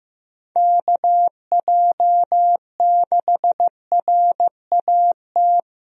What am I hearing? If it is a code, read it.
Morse code, heard as a keyed tone, about 15 wpm, 700 Hz. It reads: KJ6RAT
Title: USA Call Signs - 6 Characters - Repeated 3x - 15wpm